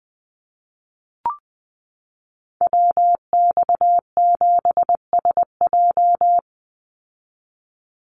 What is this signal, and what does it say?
Morse code, heard as a keyed tone, about 20 wpm, 700 Hz. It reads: WX7SJ
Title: Call Signs Repeated 2x - 20wpm